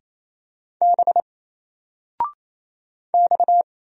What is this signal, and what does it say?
Morse code, heard as a keyed tone, about 27 wpm, 700 Hz. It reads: B X
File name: Single-Letter-Number-28wpm